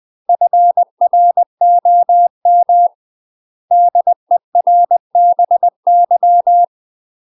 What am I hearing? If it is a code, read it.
Morse code, heard as a keyed tone, about 20 wpm, 700 Hz. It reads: FROM DERBY
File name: UK MASTER QSOs TRAINING / 20WPM / UK MASTER QSOs 0012